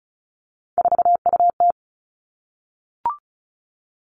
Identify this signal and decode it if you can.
Morse code, heard as a keyed tone, about 36 wpm, 700 Hz. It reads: 4UT
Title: Three Challenging Letter-Number Combinations 35wpm